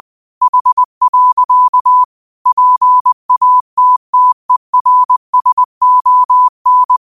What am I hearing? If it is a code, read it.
Morse code, heard as a keyed tone, about 20 wpm, 1000 Hz. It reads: H. PATTERSON